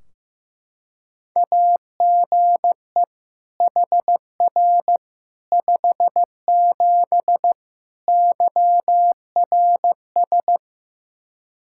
Morse code, heard as a keyed tone, about 16 wpm, 700 Hz. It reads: AGE HR 57 YRS
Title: Practice QSO - Exchange 3 - More Info - 15wpm